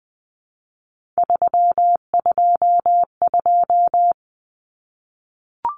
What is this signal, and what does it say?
Morse code, heard as a keyed tone, about 20 wpm, 700 Hz. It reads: 322
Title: Numbers - 3-Digit 20wpm